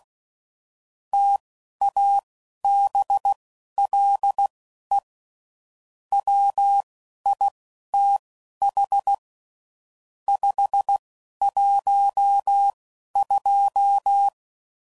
Morse code, heard as a keyed tone, about 16 wpm, spaced as for 11 wpm, 800 Hz. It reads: TABLE WITH 512
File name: CW Code Groups @10WPM / Technical Subjects / AND Gate